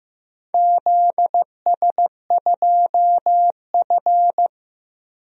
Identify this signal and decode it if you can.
Morse code, heard as a keyed tone, about 15 wpm, 700 Hz. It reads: ZS2F
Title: DXCC-100 Call Signs - 4 Characters - 15wpm